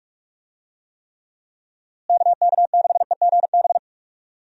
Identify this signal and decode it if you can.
Morse code, heard as a keyed tone, about 44 wpm, 700 Hz. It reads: KK6EGB
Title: USA Call Signs - 6 Characters - Repeated 3x - 45wpm